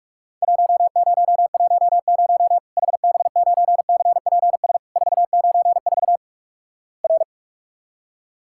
Morse code, heard as a keyed tone, about 47 wpm, 700 Hz. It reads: 1010 HB9CPS 494 R
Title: Worked All Europe DX Contest - QTC Exchange - 45wpm